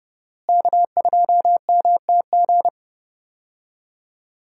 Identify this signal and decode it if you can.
Morse code, heard as a keyed tone, about 30 wpm, 700 Hz. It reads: K2MTG
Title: USA Call Signs - 5 Characters - Repeated 3x - 30wpm